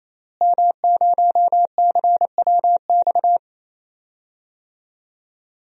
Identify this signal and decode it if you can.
Morse code, heard as a keyed tone, about 27 wpm, 700 Hz. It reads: M0CWX
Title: DXCC-100 Call Signs - 5 Characters - Repeated 3x - 28wpm